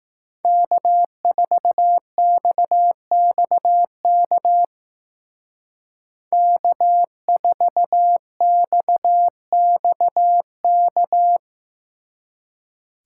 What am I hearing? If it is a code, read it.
Morse code, heard as a keyed tone, about 19 wpm, 700 Hz. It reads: K4XXK K4XXK